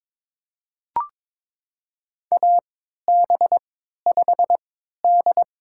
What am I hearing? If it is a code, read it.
Morse code, heard as a keyed tone, about 22 wpm, spaced as for 12 wpm, 700 Hz. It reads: AB5D
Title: USA Call Signs - 4 Characters - 3x Character Spacing - 22wpm